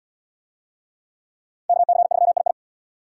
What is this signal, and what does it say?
Morse code, heard as a keyed tone, about 68 wpm, 690 Hz. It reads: KC3II